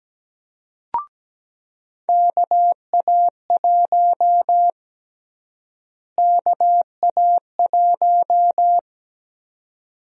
Morse code, heard as a keyed tone, about 17 wpm, 700 Hz. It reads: KA1 KA1
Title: USA Call Sign Prefixes - Repeated 2x - 17wpm